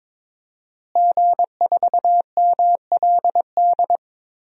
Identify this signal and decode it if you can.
Morse code, heard as a keyed tone, about 22 wpm, 700 Hz. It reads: G4MLD
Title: DXCC-100 Call Signs - 5 Characters - Repeated 3x - 22wpm